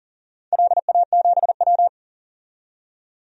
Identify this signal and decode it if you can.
Morse code, heard as a keyed tone, about 38 wpm, 700 Hz. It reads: LA7W
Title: DXCC-100 Call Signs - 4 Characters - 40wpm